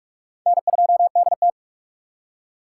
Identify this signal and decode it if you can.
Morse code, heard as a keyed tone, about 59 wpm, spaced as for 51 wpm, 700 Hz. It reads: N2DT
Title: Speed Racing - USA Call Signs 45wpm